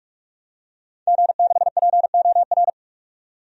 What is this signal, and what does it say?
Morse code, heard as a keyed tone, about 41 wpm, 700 Hz. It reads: G6POR
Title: DXCC-100 Call Signs - 5 Characters - 45wpm